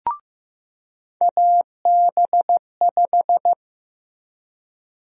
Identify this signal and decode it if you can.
Morse code, heard as a keyed tone, about 16 wpm, 700 Hz. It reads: AB5
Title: USA Call Sign Prefixes - Repeated 2x - 15wpm